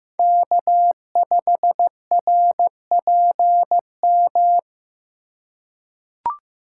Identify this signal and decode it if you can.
Morse code, heard as a keyed tone, about 15 wpm, 700 Hz. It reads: K5RPM
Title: USA Call Signs - 5 Characters - 3x Character Spacing - 15wpm